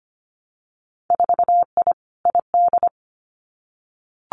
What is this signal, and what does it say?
Morse code, heard as a keyed tone, about 25 wpm, 700 Hz. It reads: 4I ID
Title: Winter Field Day Exchanges 25wpm